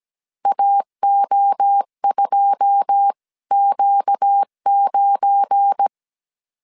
Morse code, heard as a keyed tone, about 17 wpm, 800 Hz. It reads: AO2Q9
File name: CW / DET / 1 9 1500C S C 16wpm 2909